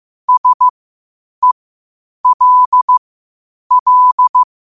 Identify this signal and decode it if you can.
Morse code, heard as a keyed tone, about 15 wpm, spaced as for 8 wpm, 1000 Hz. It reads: SELL
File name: qotdfiles / 8wpm